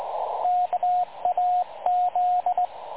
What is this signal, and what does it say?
Morse code, heard as a keyed tone, about 19 wpm, 720 Hz, with radio noise. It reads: KAZ
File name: dlazaawansowanych / nagraniazpasma / websdr recording 3565.0kHz